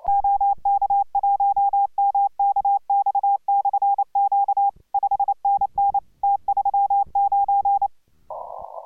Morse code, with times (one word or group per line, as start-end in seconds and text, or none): 0.1-4.7 OK1MKX/Q
4.9-6.0 5NN
6.2-7.9 T39